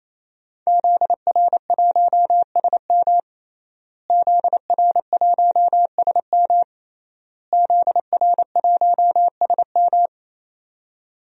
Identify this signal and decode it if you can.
Morse code, heard as a keyed tone, about 27 wpm, 700 Hz. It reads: ZR1SM ZR1SM ZR1SM
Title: DXCC-100 Call Signs - 5 Characters - Repeated 3x - 28wpm